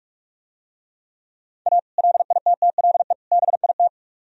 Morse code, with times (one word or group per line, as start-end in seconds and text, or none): 1.7-1.8 A
2.0-3.1 LITTLE
3.3-3.9 BIT